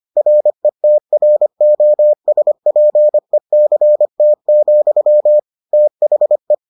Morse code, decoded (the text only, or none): RETROSPECT, THE